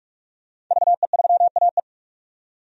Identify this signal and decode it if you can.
Morse code, heard as a keyed tone, about 48 wpm, 700 Hz. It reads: VE3AE
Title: DXCC-100 Call Signs - 5 Characters - 45wpm